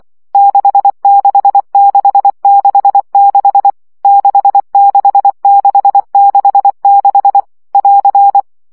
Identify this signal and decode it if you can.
Morse code, heard as a keyed tone, about 24 wpm, 800 Hz. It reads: 66666 66666 <AR>
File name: CW gyakorlas 6 20 baud